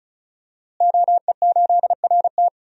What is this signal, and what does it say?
Morse code, heard as a keyed tone, about 36 wpm, 700 Hz. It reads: OE8RT
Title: DXCC-100 Call Signs - 5 Characters - Repeated 3x - 35wpm